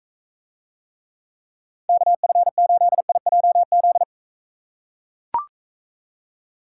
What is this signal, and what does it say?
Morse code, heard as a keyed tone, about 39 wpm, 700 Hz. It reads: KF8IJZ